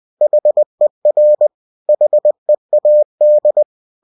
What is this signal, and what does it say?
Morse code, heard as a keyed tone, about 20 wpm, 600 Hz. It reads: HER HEAD
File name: cw-books / winnie-the-pooh / winnie-20wpm-page-0069